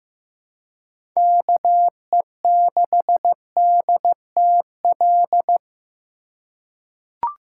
Morse code, heard as a keyed tone, about 15 wpm, 700 Hz. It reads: KE6DTL
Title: USA Call Signs - 4x Character Spacing - 15wpm